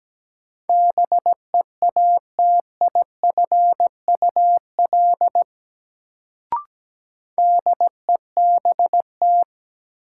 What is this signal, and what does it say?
Morse code, heard as a keyed tone, about 17 wpm, 700 Hz. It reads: BEATIFUL DEBT
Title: Sentences from Top 1300 Words - New Word Drill - 17wpm